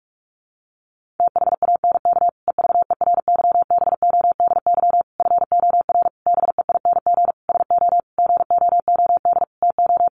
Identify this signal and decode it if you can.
Morse code, heard as a keyed tone, about 44 wpm, 700 Hz. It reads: THANK EVERYBODY FOR BEING SO GOOD TO
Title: Sentences from Top 1400 Words 45wpm